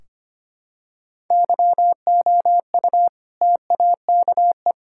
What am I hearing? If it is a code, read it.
Morse code, heard as a keyed tone, about 25 wpm, 700 Hz. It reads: YOU TAKE
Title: Sentences from Top 1600 Words 25wpm